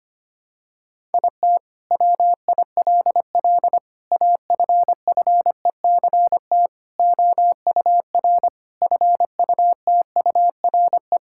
Morse code, heard as a keyed tone, about 25 wpm, 700 Hz. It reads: IT WILL AFFECT OUR FUTURE
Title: Sentences from Top 1000 Words - Review Part 6 - 25wpm